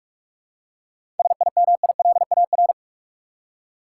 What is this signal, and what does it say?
Morse code, heard as a keyed tone, about 46 wpm, 700 Hz. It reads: SIMILAR
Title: Seven Letter Words - 45wpm